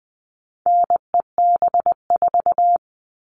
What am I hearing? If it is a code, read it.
Morse code, heard as a keyed tone, about 20 wpm, 700 Hz. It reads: NEB4
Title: All Cut Numbers - 4-Digit 20wpm